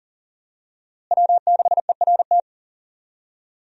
Morse code, heard as a keyed tone, about 40 wpm, 700 Hz. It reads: W6ERT
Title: USA Call Signs - 5 Characters - Repeated 2x - 40wpm